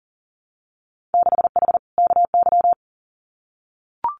Morse code, heard as a keyed tone, about 38 wpm, 700 Hz. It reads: 6H KY
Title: Winter Field Day Exchanges Encore 40wpm